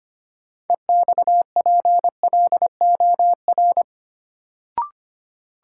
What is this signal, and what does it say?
Morse code, heard as a keyed tone, about 25 wpm, 700 Hz. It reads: EXPLOR
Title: Sentences from Top 1100 Words - New Word Drill Alt - 25wpm